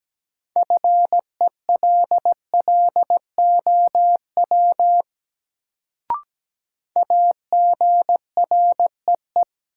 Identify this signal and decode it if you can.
Morse code, heard as a keyed tone, about 17 wpm, 700 Hz. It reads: FELLOW AGREE
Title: Sets of 5 Words Encore - New Word Drill - 17wpm